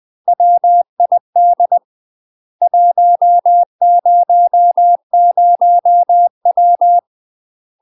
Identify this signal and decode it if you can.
Morse code, heard as a keyed tone, about 20 wpm, 700 Hz. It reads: WID 100W
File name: UK MASTER QSOs TRAINING / 20WPM / UK MASTER QSOs 0020